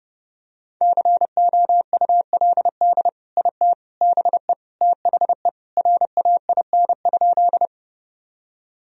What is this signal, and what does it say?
Morse code, heard as a keyed tone, about 30 wpm, 700 Hz. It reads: COULD IT BE THE RAIN?
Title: Sentences from Top 1000 Words - Review Part 1 - 30wpm